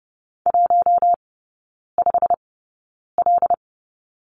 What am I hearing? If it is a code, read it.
Morse code, heard as a keyed tone, about 30 wpm, spaced as for 9 wpm, 700 Hz. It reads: 15L
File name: Lesson 19e - Letter w - Three Characters 30wpm